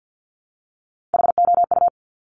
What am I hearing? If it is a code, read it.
Morse code, heard as a keyed tone, about 50 wpm, 700 Hz. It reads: HOU
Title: Top Three Letter Combinations in Words 50wpm